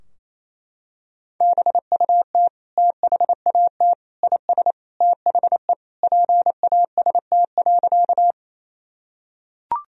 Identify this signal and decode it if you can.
Morse code, heard as a keyed tone, about 28 wpm, 700 Hz. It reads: BUT THAT IS THE PAST.
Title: Sentences from Top 1000 Words - Review Part 3 - 28wpm